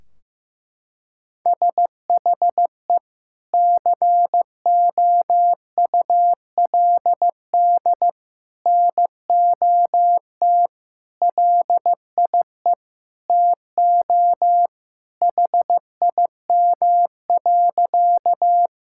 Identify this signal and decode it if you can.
Morse code, heard as a keyed tone, about 15 wpm, 700 Hz. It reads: SHE COULD NOT LIE TO HIM.
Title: Sentences from Top 1600 Words 15wpm